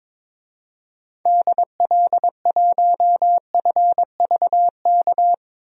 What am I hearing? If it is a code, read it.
Morse code, heard as a keyed tone, about 22 wpm, 700 Hz. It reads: DL1FVK
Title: DXCC-100 Call Signs - 6 Characters - 22wpm